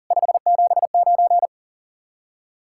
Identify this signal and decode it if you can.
Morse code, heard as a keyed tone, about 49 wpm, 700 Hz. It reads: 579
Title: Speed-Racing Common QSO Phrases Encore - 40wpm